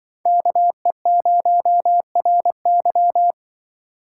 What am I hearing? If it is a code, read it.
Morse code, heard as a keyed tone, about 25 wpm, 700 Hz. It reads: KE0RY
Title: Speed Racing - USA Call Signs 20wpm